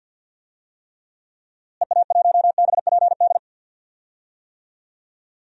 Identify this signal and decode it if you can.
Morse code, heard as a keyed tone, about 50 wpm, 700 Hz. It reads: EA1BPD
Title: DXCC-100 Call Signs - 6 Characters - Repeated 2x - 50wpm